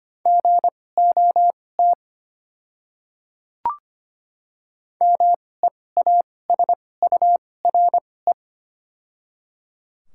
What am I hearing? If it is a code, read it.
Morse code, heard as a keyed tone, about 25 wpm, spaced as for 18 wpm, 700 Hz. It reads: GOT MEASURE
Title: 2x Character Spacing - Top 500 Words - 25wpm